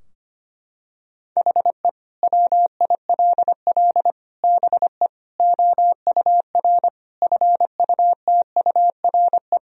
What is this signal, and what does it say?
Morse code, heard as a keyed tone, about 25 wpm, 700 Hz. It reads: HE WILL BE OUR FUTURE
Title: Sentences from Top 1000 Words - Review Part 3 - 25wpm